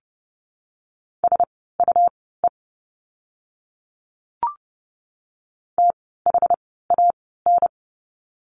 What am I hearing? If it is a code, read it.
Morse code, heard as a keyed tone, about 30 wpm, spaced as for 17 wpm, 700 Hz. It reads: SUE THAN
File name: Lesson 17f - Letter u - Words 30wpm